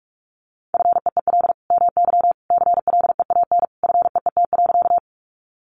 Morse code, heard as a keyed tone, about 44 wpm, 700 Hz. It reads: FEEL MY CLEAN FEET.
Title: Sentences from Top 1100 Words 45wpm